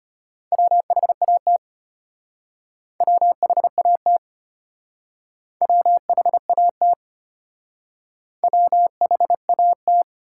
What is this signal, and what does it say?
Morse code, heard as a keyed tone, about 32 wpm, 700 Hz. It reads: WHAT WHAT WHAT WHAT